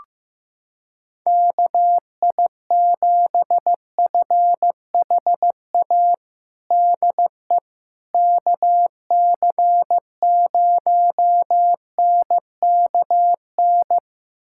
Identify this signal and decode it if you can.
Morse code, heard as a keyed tone, about 15 wpm, 700 Hz. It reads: KI7FHA DE KC0NKN
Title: USA Callsign DE Callsign 15wpm